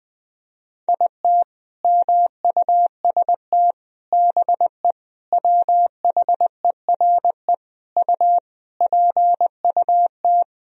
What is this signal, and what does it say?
Morse code, heard as a keyed tone, about 20 wpm, 700 Hz. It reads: IT MUST BE WHERE U PUT
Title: Sentences from Top 200 Words and Abbreviations 20wpm